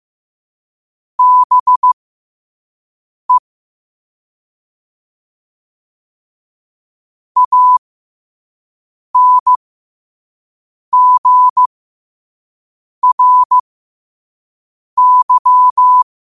Morse code, heard as a keyed tone, about 15 wpm, spaced as for 5 wpm, 1000 Hz. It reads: BE ANGRY